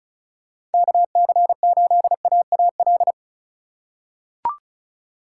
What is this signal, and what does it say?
Morse code, heard as a keyed tone, about 36 wpm, 700 Hz. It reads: KC8AAL
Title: USA Call Signs - 4x Character Spacing - 35wpm